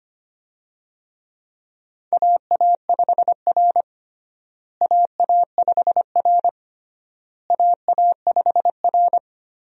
Morse code, heard as a keyed tone, about 25 wpm, 700 Hz. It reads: AA5R AA5R AA5R
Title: USA Call Signs - 4 Characters - Repeated 3x - 25wpm